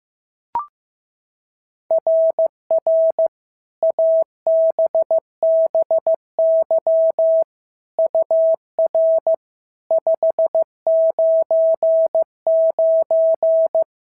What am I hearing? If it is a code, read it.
Morse code, heard as a keyed tone, about 15 wpm, 690 Hz. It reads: RR ABBY UR 599
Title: SOTA QSO Practice 15wpm